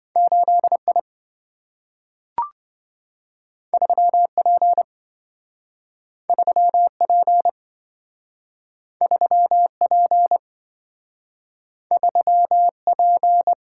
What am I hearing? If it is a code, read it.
Morse code, heard as a keyed tone, about 27 wpm, 700 Hz. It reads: 8I 3P 3P 3P 3P